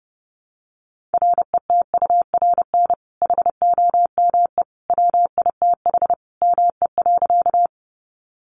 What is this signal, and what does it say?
Morse code, heard as a keyed tone, about 30 wpm, 700 Hz. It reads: RETURN HOME WITH ME.
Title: Sentences from Top 800 Words 30wpm